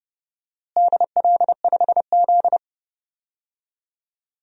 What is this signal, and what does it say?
Morse code, heard as a keyed tone, about 32 wpm, 700 Hz. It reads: DL5Z
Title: DXCC-100 Call Signs - 4 Characters - Repeated 2x - 30wpm